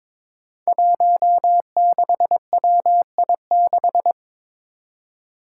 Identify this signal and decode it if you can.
Morse code, heard as a keyed tone, about 22 wpm, 700 Hz. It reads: 16WI6
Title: Five Challenging Letter-Number Combinations 22wpm